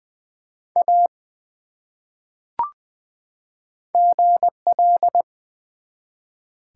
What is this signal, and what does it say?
Morse code, heard as a keyed tone, about 20 wpm, 700 Hz. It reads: A GL
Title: Top 180 QSO Elements 20wpm